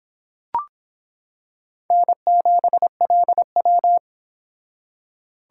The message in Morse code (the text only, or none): N7LW